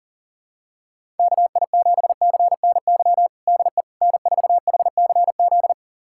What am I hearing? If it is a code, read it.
Morse code, heard as a keyed tone, about 40 wpm, 700 Hz. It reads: KI7CNY DE N4HCZ
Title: USA Callsign DE Callsign 40wpm